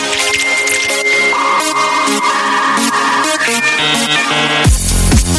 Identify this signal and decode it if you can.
Dial-up modem handshake.